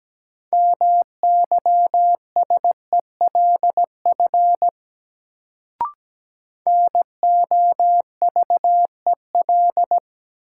Morse code, heard as a keyed tone, about 17 wpm, 700 Hz. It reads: MYSELF NOVEL
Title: Sentences from Top 1300 Words - New Word Drill Alt - 17wpm